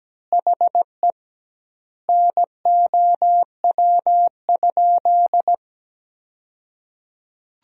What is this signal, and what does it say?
Morse code, heard as a keyed tone, about 17 wpm, 700 Hz. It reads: HE NOW?
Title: 2x Word-Spacing - Sentences from Top 200 Words - 17wpm